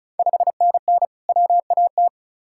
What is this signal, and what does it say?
Morse code, heard as a keyed tone, about 36 wpm, 700 Hz. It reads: 5NN WAT
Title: Ontario QSO Party Exchange 35wpm